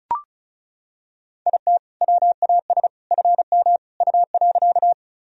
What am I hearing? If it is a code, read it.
Morse code, heard as a keyed tone, about 36 wpm, 700 Hz. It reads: IT WAS FM U.
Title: Sentences from Top 100 Words and Abbreviations 35wpm